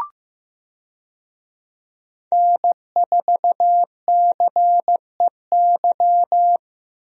Morse code, heard as a keyed tone, about 16 wpm, 700 Hz. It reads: N4CEY
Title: USA Call Signs - 5 Characters - Repeated 2x - 15wpm